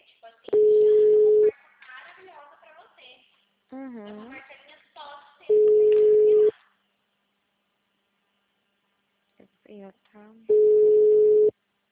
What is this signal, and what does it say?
Ringback tone (Europe, 425 Hz), with speech.